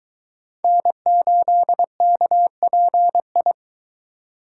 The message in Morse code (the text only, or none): N8KPI